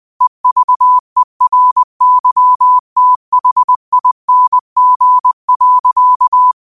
Morse code, heard as a keyed tone, about 20 wpm, 1000 Hz. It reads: EVERYTHING.
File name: qotdfiles / 20wpm